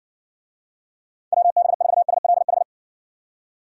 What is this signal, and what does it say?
Morse code, heard as a keyed tone, about 60 wpm, 700 Hz. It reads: WB4SLH